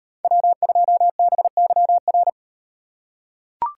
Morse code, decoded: W2BYR

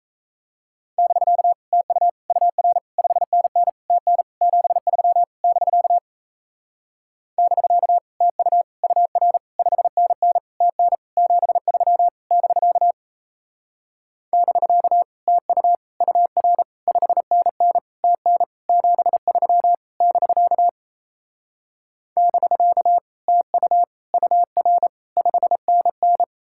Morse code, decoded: <BK> TU UR 5NN TN 73 <BK> <BK> TU UR 5NN TN 73 <BK> <BK> TU UR 5NN TN 73 <BK> <BK> TU UR 5NN

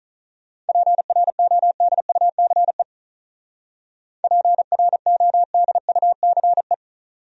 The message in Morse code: PRODUCE PRODUCE

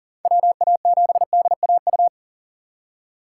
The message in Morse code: WA7DAU

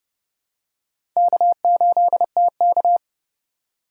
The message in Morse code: K8TK

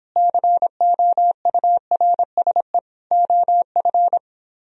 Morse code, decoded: COURSE OF

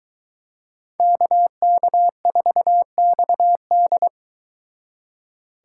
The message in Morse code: KK4XD

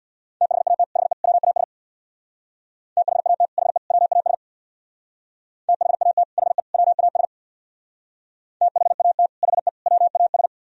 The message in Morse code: THAT HE WAS THAT HE WAS THAT HE WAS THAT HE WAS

